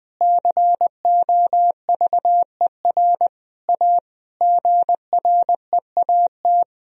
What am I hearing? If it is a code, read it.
Morse code, heard as a keyed tone, about 21 wpm, 700 Hz. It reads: COVER A GREAT
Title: Sentences from Top 1000 Words - Review Part 2 - 20wpm